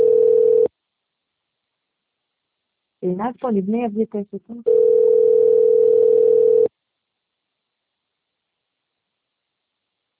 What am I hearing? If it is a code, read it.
Ringback tone (North America), 440 + 480 Hz, with speech.
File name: out- - - - - .75225